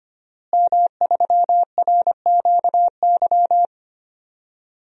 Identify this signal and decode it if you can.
Morse code, heard as a keyed tone, about 25 wpm, 700 Hz. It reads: M3RQY
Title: DXCC-100 Call Signs - 5 Characters - Repeated 2x - 25wpm